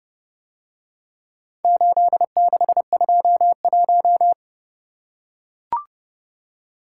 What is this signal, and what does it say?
Morse code, heard as a keyed tone, about 29 wpm, 700 Hz. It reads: 8621